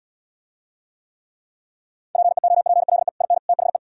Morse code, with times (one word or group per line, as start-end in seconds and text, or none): 2.2-3.1 PEOPLE
3.2-3.4 IN
3.5-3.8 THE